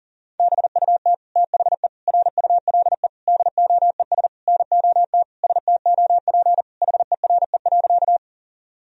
Morse code, heard as a keyed tone, about 37 wpm, 700 Hz. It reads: BUT THE RULE DOES NOT STOP HERE.
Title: Sentences from Top 1000 Words - Review Part 2 - 40wpm